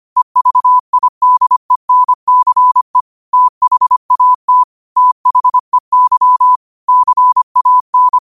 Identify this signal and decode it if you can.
Morse code, heard as a keyed tone, about 25 wpm, 1000 Hz. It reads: EVIDENCE THAT THEY CAN